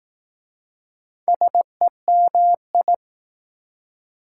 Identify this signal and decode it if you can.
Morse code, heard as a keyed tone, about 20 wpm, 700 Hz. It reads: SEMI